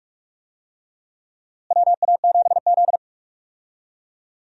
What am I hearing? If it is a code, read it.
Morse code, heard as a keyed tone, about 45 wpm, 700 Hz. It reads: WA7Z
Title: USA Call Signs - 4 Characters - Repeated 2x - 45wpm